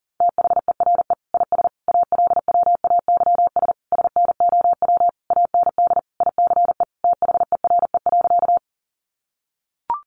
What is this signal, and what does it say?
Morse code, heard as a keyed tone, about 37 wpm, 700 Hz. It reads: THERE IS ALWAYS SNOW AND ICE THERE.
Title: Sentences from Top 1000 Words 40wpm